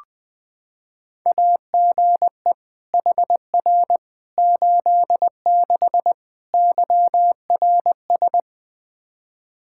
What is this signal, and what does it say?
Morse code, heard as a keyed tone, about 20 wpm, 700 Hz. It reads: AGE HR 86 YRS
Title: Practice QSO - Exchange 3 - More Info - 20wpm